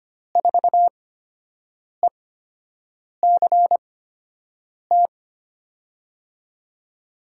Morse code, heard as a keyed tone, about 25 wpm, spaced as for 7 wpm, 700 Hz. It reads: 4ECT
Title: USA Call Signs - 8x Character Spacing - 25wpm